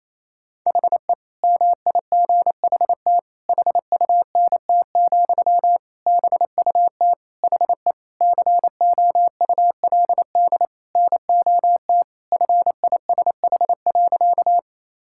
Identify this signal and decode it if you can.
Morse code, heard as a keyed tone, about 28 wpm, 700 Hz. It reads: HE MIGHT HUNT, BUT HE COULD NOT FISH.
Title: Sentences from Top 1600 Words 28wpm